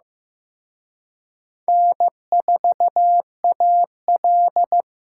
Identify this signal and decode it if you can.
Morse code, heard as a keyed tone, about 18 wpm, 700 Hz. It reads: N4AL
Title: Speed Racing - USA 1x2 Call Signs 15wpm